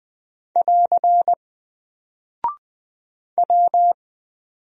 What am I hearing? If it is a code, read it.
Morse code, heard as a keyed tone, about 20 wpm, 700 Hz. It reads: <AR> W
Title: Single Character (Letter:Number:Punctuation:Prosign) 20wpm